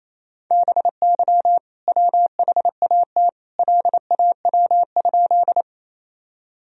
Morse code, heard as a keyed tone, about 28 wpm, 700 Hz. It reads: BY WHAT LAW?
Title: Sentences from Top 1000 Words - Review Part 3 - 28wpm